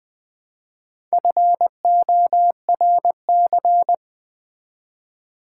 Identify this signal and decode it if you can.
Morse code, heard as a keyed tone, about 20 wpm, 700 Hz. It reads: FORC
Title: Build A Word - 1 Letter At A Time - 3 to 5-Letter Words - 20wpm